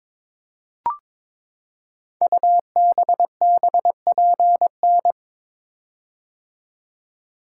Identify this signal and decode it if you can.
Morse code, heard as a keyed tone, about 22 wpm, 700 Hz. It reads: UBBPN
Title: Code Groups - 5 Letter:Numbers - 22wpm